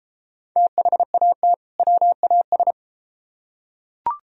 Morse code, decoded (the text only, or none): THAT WAS